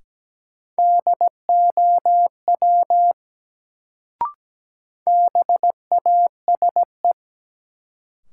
Morse code, heard as a keyed tone, about 17 wpm, 700 Hz. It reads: DOW BASE
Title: Sets of 5 Words - New Word Drill - 17wpm